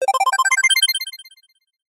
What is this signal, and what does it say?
There is no code.